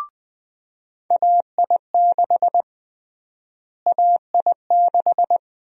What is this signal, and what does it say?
Morse code, heard as a keyed tone, about 21 wpm, 700 Hz. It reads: AI6 AI6
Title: USA Call Sign Prefixes - Repeated 2x - 20wpm